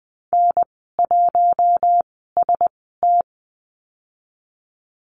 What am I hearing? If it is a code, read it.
Morse code, heard as a keyed tone, about 20 wpm, spaced as for 15 wpm, 700 Hz. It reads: N1ST
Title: 048 Lesson 9g - Number 4 - Call Signs 20wpm